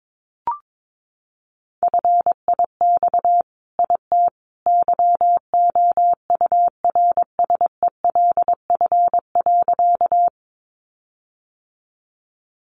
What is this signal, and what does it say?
Morse code, heard as a keyed tone, about 22 wpm, 700 Hz. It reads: FIX IT YOURSELF.
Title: Sentences from Top 1400 Words 22wpm